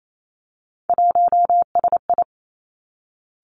Morse code, heard as a keyed tone, about 28 wpm, 700 Hz. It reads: 1SI
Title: Three Challenging Letter-Number Combinations 28wpm